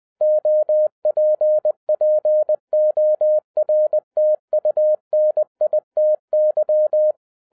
Morse code, heard as a keyed tone, about 20 wpm, 610 Hz. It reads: OPPORTUNITY